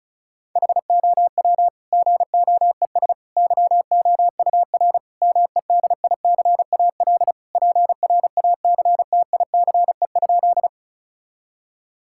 Morse code, heard as a keyed tone, about 36 wpm, 700 Hz. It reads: HOW GOES YOUR MEDICAL PRACTICE?
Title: Sentences from Top 1000 Words - Review Part 5 - 35wpm